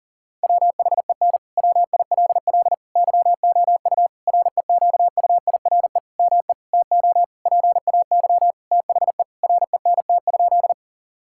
Morse code, heard as a keyed tone, about 37 wpm, 700 Hz. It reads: WHEN WILL YOU REQUIRE ME TO PAY THE RENT?
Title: Sentences from Top 1000 Words - Review Part 6 - 40wpm